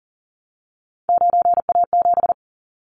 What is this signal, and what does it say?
Morse code, heard as a keyed tone, about 38 wpm, 700 Hz. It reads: 9A7
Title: Callsign Prefixes for Top 50 DXCC - 40wpm